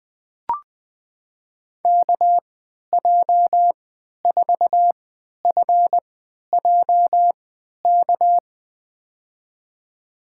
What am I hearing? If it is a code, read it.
Morse code, heard as a keyed tone, about 20 wpm, spaced as for 11 wpm, 700 Hz. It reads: KJ4FJK
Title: USA Call Signs - 6 Characters - 3x Character Spacing - 20wpm